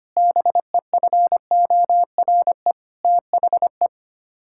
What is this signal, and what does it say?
Morse code, heard as a keyed tone, about 25 wpm, 700 Hz. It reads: BEFORE THE
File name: Sets of 9 Words - 25wpm